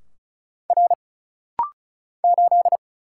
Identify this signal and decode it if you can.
Morse code, heard as a keyed tone, about 36 wpm, 700 Hz. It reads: R 8
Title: Single Letter-Number - Rapid-Fire 35wpm